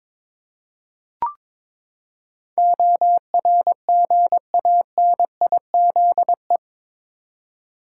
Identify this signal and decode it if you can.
Morse code, heard as a keyed tone, about 21 wpm, 700 Hz. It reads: ORGANIZE
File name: practice / 22wpm / Eight Letter Words - 22wpm